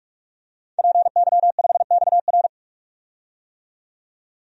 Morse code, heard as a keyed tone, about 42 wpm, 700 Hz. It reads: PY5XR
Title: DXCC-100 Call Signs - 5 Characters - 45wpm